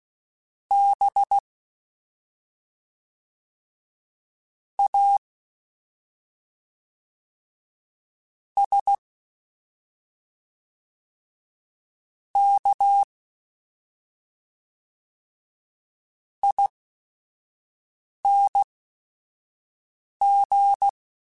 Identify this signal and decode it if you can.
Morse code, heard as a keyed tone, about 16 wpm, spaced as for 5 wpm, 800 Hz. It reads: B A S K ING
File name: Special INGs / Special INGs-mp3s / SlowToFaster ing @5